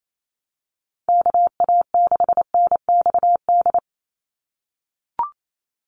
Morse code, heard as a keyed tone, about 28 wpm, 700 Hz. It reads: KA6NXD
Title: USA Call Signs 28wpm